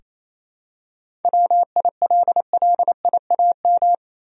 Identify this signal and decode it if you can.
Morse code, heard as a keyed tone, about 28 wpm, 700 Hz. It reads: WILLIAM
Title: USA Common Names - 28wpm